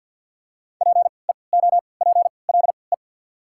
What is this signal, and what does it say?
Morse code, heard as a keyed tone, about 53 wpm, spaced as for 29 wpm, 690 Hz. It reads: PEOPLE
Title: 3x Character Spacing - Top 100 Words - 50wpm